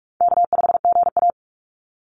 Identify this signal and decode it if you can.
Morse code, heard as a keyed tone, about 44 wpm, 700 Hz. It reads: K5GA
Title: Call Signs - 45wpm